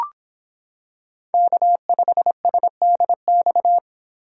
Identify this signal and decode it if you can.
Morse code, heard as a keyed tone, about 26 wpm, 700 Hz. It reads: K5SDX